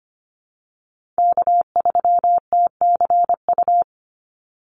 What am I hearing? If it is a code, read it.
Morse code, heard as a keyed tone, about 25 wpm, 700 Hz. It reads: K3TCU
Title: NA SOTA Chaser Callsigns 25wpm